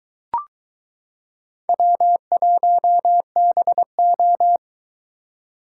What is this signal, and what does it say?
Morse code, heard as a keyed tone, about 22 wpm, 700 Hz. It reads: W1BO